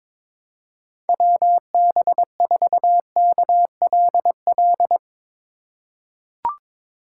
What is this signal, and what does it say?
Morse code, heard as a keyed tone, about 22 wpm, 700 Hz. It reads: WB4KLL
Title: USA Call Signs - 6 Characters - 3x Character Spacing - 22wpm